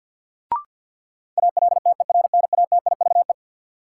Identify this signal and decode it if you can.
Morse code, heard as a keyed tone, about 51 wpm, 700 Hz. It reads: ALTERNATIVE
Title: Sentences from Top 1000 Words - Review Part 4 - New Word Drill - 50wpm